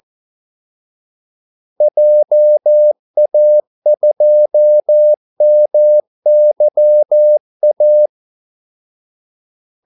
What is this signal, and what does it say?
Morse code, heard as a keyed tone, about 14 wpm, 600 Hz. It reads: JA2MYA
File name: Callsigns-14wpm